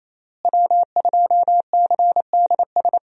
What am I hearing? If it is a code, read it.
Morse code, heard as a keyed tone, about 27 wpm, 700 Hz. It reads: W2CDS
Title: USA Call Signs - 5 Characters - Repeated 3x - 28wpm